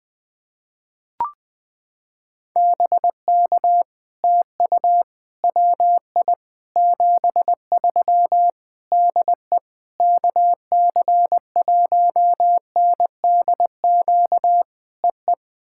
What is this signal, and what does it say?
Morse code, heard as a keyed tone, about 21 wpm, 700 Hz. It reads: BK TU WI 73 DE KC1NDQ EE
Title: POTA QSO Practice - 20wpm